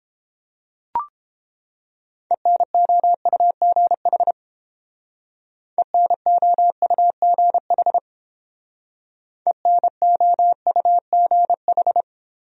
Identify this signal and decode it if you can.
Morse code, heard as a keyed tone, about 29 wpm, 700 Hz. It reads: ENOUGH ENOUGH ENOUGH